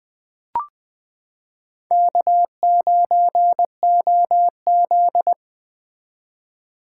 Morse code, heard as a keyed tone, about 21 wpm, 700 Hz. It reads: K9OZ
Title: 2023 SOTA Chaser Call Signs - 20wpm